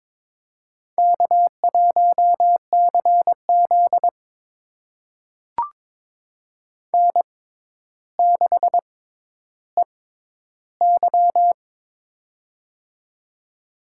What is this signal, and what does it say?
Morse code, heard as a keyed tone, about 22 wpm, spaced as for 11 wpm, 700 Hz. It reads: K1CZ N6EY
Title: USA Call Signs - 4 Characters - 6x Character Spacing - 22wpm